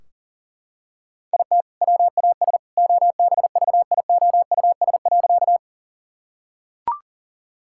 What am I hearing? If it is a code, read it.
Morse code, heard as a keyed tone, about 40 wpm, 700 Hz. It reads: IT WAS OBVIOUS.
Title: Sentences from Top 1600 Words 40wpm